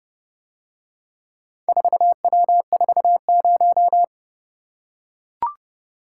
Morse code, heard as a keyed tone, about 30 wpm, 700 Hz. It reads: 4W40